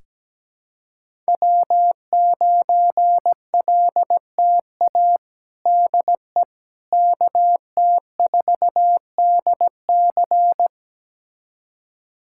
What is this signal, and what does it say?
Morse code, heard as a keyed tone, about 17 wpm, 700 Hz. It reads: W9LTA DE KT4DC
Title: USA Callsign DE Callsign - 5 Characters - 17wpm